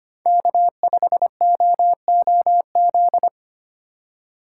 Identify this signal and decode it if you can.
Morse code, heard as a keyed tone, about 30 wpm, 700 Hz. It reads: K5OOZ
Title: Speed Racing - USA 1x3 Call Signs 25wpm